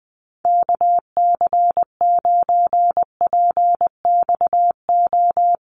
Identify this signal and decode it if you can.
Morse code, heard as a keyed tone, about 20 wpm, 700 Hz. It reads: KC9PXO
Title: USA Call Signs Repeated 3x 20wpm